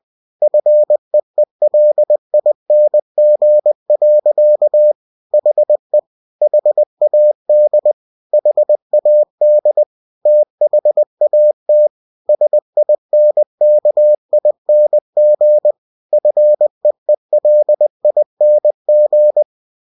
Morse code, heard as a keyed tone, about 20 wpm, 600 Hz. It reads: FEELING. HE HAD HAD THAT SINKING FEELING
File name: cw-books / winnie-the-pooh / winnie-20wpm-page-0038